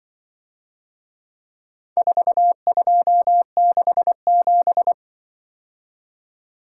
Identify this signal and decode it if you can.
Morse code, heard as a keyed tone, about 23 wpm, 700 Hz. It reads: 4267
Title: Speed-Racing - Four Letter-Number Combinations - 20wpm